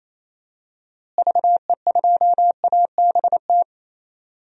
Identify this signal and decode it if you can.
Morse code, heard as a keyed tone, about 28 wpm, 700 Hz. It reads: VE2ABT
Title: DXCC-100 Call Signs - 6 Characters - Repeated 3x - 28wpm